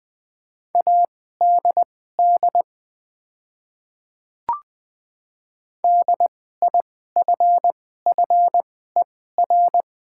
Morse code, heard as a keyed tone, about 20 wpm, spaced as for 14 wpm, 700 Hz. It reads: ADD DIFFER
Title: 2x Character Spacing - Top 300 Words - 20wpm